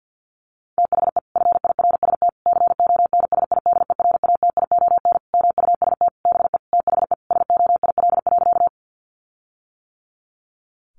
Morse code, heard as a keyed tone, about 50 wpm, 700 Hz. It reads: THE FIRST CONSIDERATION MUST BE THE SOIL.
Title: Sentences from Top 1500 Words 50wpm